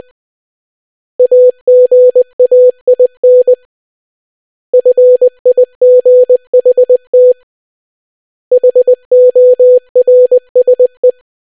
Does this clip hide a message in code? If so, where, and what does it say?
Yes: Morse code from 1.2 to 11.1 s, heard as a keyed tone, about 20 wpm, 500 Hz. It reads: AGAIN FIGHT HORSE